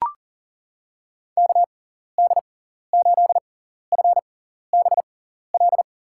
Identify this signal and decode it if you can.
Morse code, heard as a keyed tone, about 40 wpm, spaced as for 14 wpm, 700 Hz. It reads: KD8FBL
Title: USA Call Signs - 6 Characters - 6x Character Spacing - 40wpm